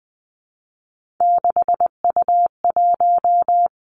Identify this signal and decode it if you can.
Morse code, heard as a keyed tone, about 20 wpm, 700 Hz. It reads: 6U1
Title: Three Challenging Letter-Number Combinations 20wpm